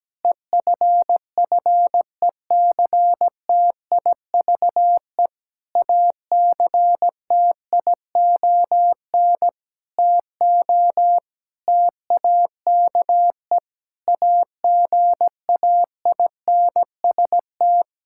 Morse code, heard as a keyed tone, about 17 wpm, 700 Hz. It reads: EFFECTIVE ACTION TO TAKE AGAINST
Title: Sentences from Top 1000 Words - Review Part 6 - 17wpm